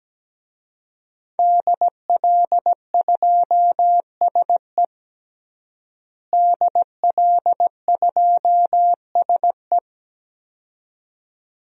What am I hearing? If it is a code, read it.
Morse code, heard as a keyed tone, about 17 wpm, 700 Hz. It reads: DL2SE DL2SE
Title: DXCC-100 Call Signs - 5 Characters - Repeated 2x - 17wpm